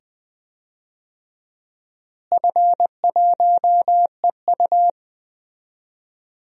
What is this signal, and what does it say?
Morse code, heard as a keyed tone, about 20 wpm, 700 Hz. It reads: F1EU
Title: DXCC-100 Call Signs - 4 Characters - Repeated 2x - 20wpm